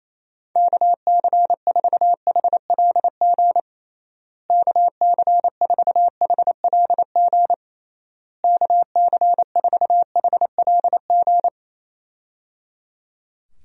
Morse code, heard as a keyed tone, about 27 wpm, 700 Hz. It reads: KC4HLG KC4HLG KC4HLG
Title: USA Call Signs - 6 Characters - Repeated 3x - 28wpm